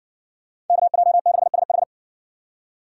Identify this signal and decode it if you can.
Morse code, heard as a keyed tone, about 56 wpm, 700 Hz. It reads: KJ6SH